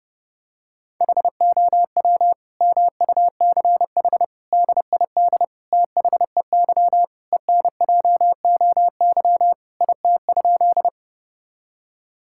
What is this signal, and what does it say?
Morse code, heard as a keyed tone, about 32 wpm, 700 Hz. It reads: HOW MUCH DID THEY ENJOY IT?
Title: Sentences from Top 1000 Words - Review Part 5 - 30wpm